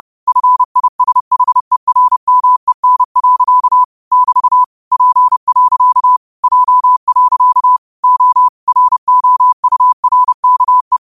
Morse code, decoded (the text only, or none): FISHERMEN. <BT> P. J. O ROURKE